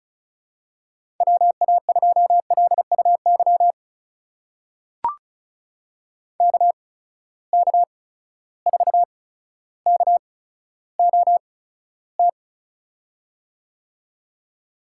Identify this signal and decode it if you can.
Morse code, heard as a keyed tone, about 34 wpm, spaced as for 15 wpm, 700 Hz. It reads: WA2LUY KK4KOT
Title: USA Call Signs - 6 Characters - 8x Character Spacing - 35wpm